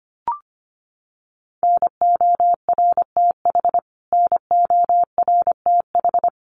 Morse code, heard as a keyed tone, about 24 wpm, 700 Hz. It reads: NORTH NORTH
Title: Top 500 Words Repeated 25wpm